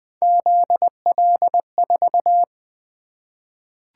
Morse code, heard as a keyed tone, about 20 wpm, 700 Hz. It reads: ZL4
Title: Three Letter-Number Combinations - 20wpm